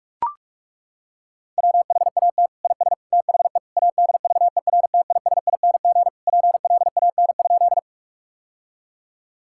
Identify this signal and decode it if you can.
Morse code, heard as a keyed tone, about 41 wpm, 700 Hz. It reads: WHAT IS THE ADVERTISING PLAN?
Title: Sentences from Top 1000 Words - Review Part 6 - 45wpm